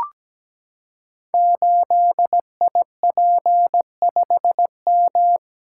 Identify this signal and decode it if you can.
Morse code, heard as a keyed tone, about 17 wpm, 700 Hz. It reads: 8IP5M
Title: Code Groups - 5 Letter:Numbers - 17wpm